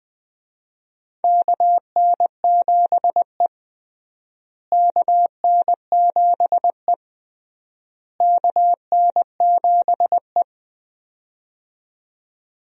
Morse code, heard as a keyed tone, about 20 wpm, 700 Hz. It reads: KN7E KN7E KN7E
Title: USA Call Signs - 4 Characters - Repeated 3x - 20wpm